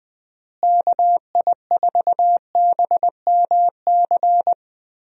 Morse code, heard as a keyed tone, about 20 wpm, 700 Hz. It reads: KI4BMC
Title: USA Call Signs - 6 Characters - Repeated 3x - 20wpm